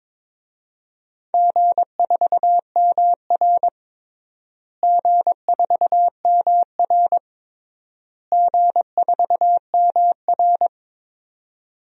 Morse code, heard as a keyed tone, about 22 wpm, 700 Hz. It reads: G4MR G4MR G4MR
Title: DXCC-100 Call Signs - 4 Characters - Repeated 3x - 22wpm